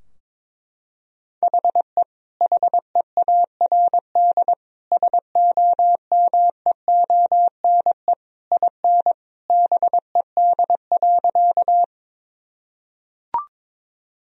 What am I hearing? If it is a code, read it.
Morse code, heard as a keyed tone, about 22 wpm, 700 Hz. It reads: HE HEARD SOMEONE IN BED.
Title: Sentences from Top 1000 Words - Review Part 4 - 22wpm